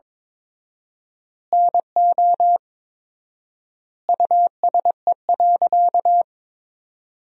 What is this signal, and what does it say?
Morse code, heard as a keyed tone, about 22 wpm, 700 Hz. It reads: NO USE.
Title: 4x Word-Spacing - Sentences from Top 100 Words - 22wpm